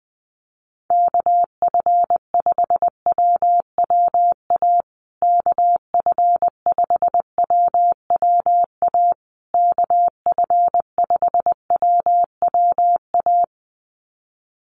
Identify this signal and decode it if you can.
Morse code, heard as a keyed tone, about 20 wpm, 700 Hz. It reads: KF5WWA KF5WWA KF5WWA
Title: USA Call Signs Repeated 3x 20wpm